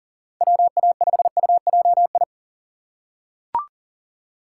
Morse code, heard as a keyed tone, about 37 wpm, 700 Hz. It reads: WA5UJI
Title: USA Call Signs - 6 Characters - 3x Character Spacing - 40wpm